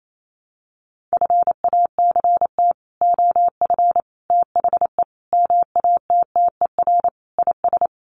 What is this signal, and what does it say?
Morse code, heard as a keyed tone, about 28 wpm, 700 Hz. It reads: FACT OF THE MATTER IS
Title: Sets of 5 Words 28wpm